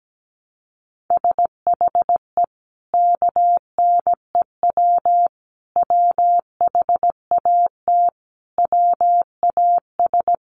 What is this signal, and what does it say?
Morse code, heard as a keyed tone, about 17 wpm, 700 Hz. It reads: SHE KNEW WHAT WAS
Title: Sentences from Top 400 Words 17wpm